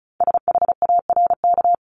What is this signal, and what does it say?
Morse code, heard as a keyed tone, about 36 wpm, 700 Hz. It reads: SHARK
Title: Multi-letter Phonemes 35wpm